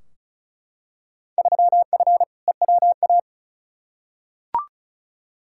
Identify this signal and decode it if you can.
Morse code, heard as a keyed tone, about 34 wpm, 700 Hz. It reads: 3F EWA
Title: Field Day Exchanges Encore - 35wpm